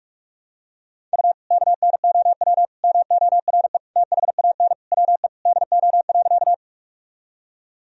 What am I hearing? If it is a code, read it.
Morse code, heard as a keyed tone, about 46 wpm, 700 Hz. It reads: U KNOW MORE THAN WE DO.
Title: Sentences from Top 100 Words and Abbreviations 45wpm